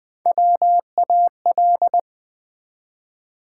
Morse code, heard as a keyed tone, about 20 wpm, 700 Hz. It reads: WAL
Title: Build A Word - 1 Letter At A Time - 4-Letter Words - 20wpm